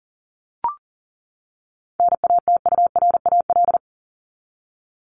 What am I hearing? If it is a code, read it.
Morse code, heard as a keyed tone, about 40 wpm, 700 Hz. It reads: NATURAL
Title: LY Suffix - Top 100 Words with Base Words 40wpm